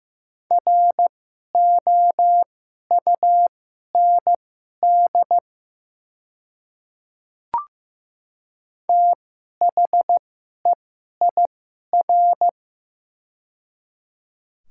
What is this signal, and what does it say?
Morse code, heard as a keyed tone, about 15 wpm, spaced as for 11 wpm, 700 Hz. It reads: ROUND THEIR
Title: 2x Character Spacing - Top 300 Words - 15wpm